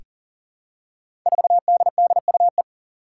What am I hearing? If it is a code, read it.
Morse code, heard as a keyed tone, about 40 wpm, 700 Hz. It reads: 4DDUE
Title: Five Challenging Letter-Number Combinations 40wpm